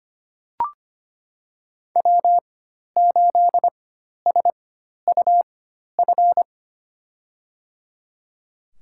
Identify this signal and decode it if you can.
Morse code, heard as a keyed tone, about 25 wpm, spaced as for 12 wpm, 700 Hz. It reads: W8SUF
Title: USA Call Signs - 5 Characters - 4x Character Spacing - 25wpm